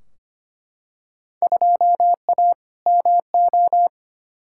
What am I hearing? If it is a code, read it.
Morse code, heard as a keyed tone, about 25 wpm, 700 Hz. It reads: 2A MO
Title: Field Day Exchanges Encore - 25wpm